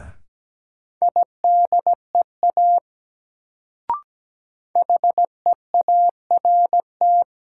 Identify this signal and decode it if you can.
Morse code, heard as a keyed tone, about 17 wpm, 700 Hz. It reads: IDEA HEART
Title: Sets of 3 Words - New Word Drill - 17wpm